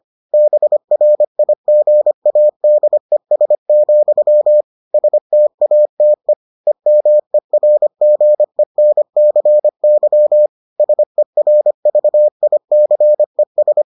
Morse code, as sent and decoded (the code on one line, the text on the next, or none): -... .-. .. --. .- -.. ... --..-- / ... - .- - . / . -- . .-. --. -. -.-. -.-- / ... .-. ...- .. -.-. ...
BRIGADES, STATE EMERGENCY SERVICES